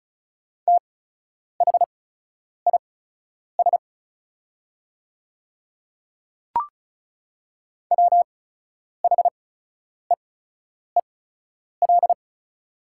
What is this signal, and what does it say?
Morse code, heard as a keyed tone, about 36 wpm, spaced as for 10 wpm, 690 Hz. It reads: THIS WHEEL
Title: 8x Character Spacing - Top 500 Words - 35wpm